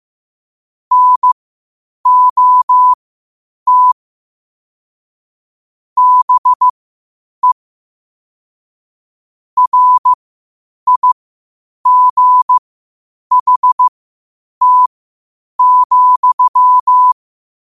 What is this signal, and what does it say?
Morse code, heard as a keyed tone, about 15 wpm, spaced as for 8 wpm, 1000 Hz. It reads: NOT BE RIGHT,
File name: qotdfiles / 8wpm